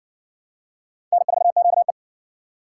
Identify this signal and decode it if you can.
Morse code, heard as a keyed tone, about 54 wpm, 700 Hz. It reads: N4QE